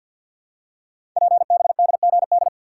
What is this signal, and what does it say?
Morse code, heard as a keyed tone, about 51 wpm, 690 Hz. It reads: PBDGD
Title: Code Groups - 5 Letters - 50wpm